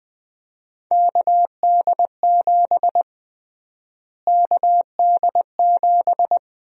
Morse code, heard as a keyed tone, about 20 wpm, 700 Hz. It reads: KD7 KD7
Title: USA Call Sign Prefixes - Repeated 2x - 20wpm